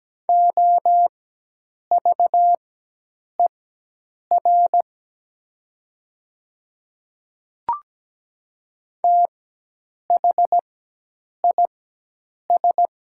Morse code, heard as a keyed tone, about 17 wpm, spaced as for 8 wpm, 700 Hz. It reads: OVER THIS
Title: 4x Character Spacing - Top 100 Words - 17wpm